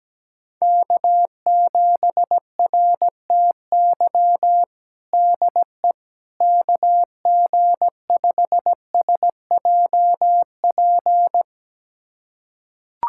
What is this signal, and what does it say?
Morse code, heard as a keyed tone, about 17 wpm, 700 Hz. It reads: K7RTY DE KG5SJP
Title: USA Callsign DE Callsign 17wpm